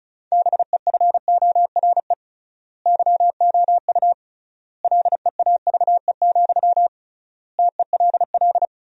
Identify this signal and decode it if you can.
Morse code, heard as a keyed tone, about 36 wpm, 700 Hz. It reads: BEFORE YOU LEAVE, TELL
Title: 3x Word-Spacing - Sentences from Top 400 Words - 35wpm